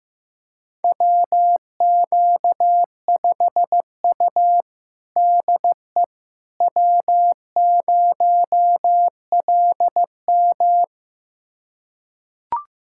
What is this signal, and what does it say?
Morse code, heard as a keyed tone, about 16 wpm, 700 Hz. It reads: WQ5U DE W0LM
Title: USA Callsign DE Callsign - 4 Characters - 15wpm